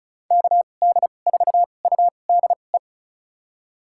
Morse code, heard as a keyed tone, about 34 wpm, spaced as for 25 wpm, 700 Hz. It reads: KD4UDE
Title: USA Call Signs - 6 Characters - 2x Character Spacing - 35wpm